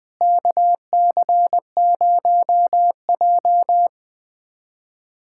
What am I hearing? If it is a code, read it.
Morse code, heard as a keyed tone, about 20 wpm, 700 Hz. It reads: KC0J
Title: USA Callsign DE Callsign - 4 Characters - 20wpm